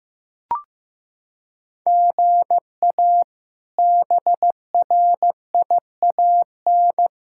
Morse code, heard as a keyed tone, about 15 wpm, 700 Hz. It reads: GA BRIAN
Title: K1USN SST QSO Practice - 15wpm